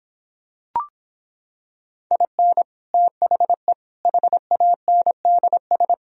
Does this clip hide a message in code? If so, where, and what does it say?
Yes: Morse code from 2.1 to 5.9 s, heard as a keyed tone, about 26 wpm, 700 Hz. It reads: IN THE HANDS